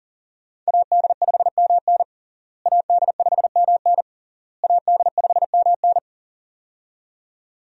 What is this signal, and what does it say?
Morse code, heard as a keyed tone, about 40 wpm, 700 Hz. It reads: AD5MN AD5MN AD5MN
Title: USA Call Signs - 5 Characters - Repeated 3x - 40wpm